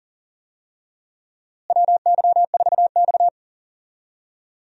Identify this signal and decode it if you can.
Morse code, heard as a keyed tone, about 40 wpm, 700 Hz. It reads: WY4X
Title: USA Call Signs - 4 Characters - Repeated 2x - 40wpm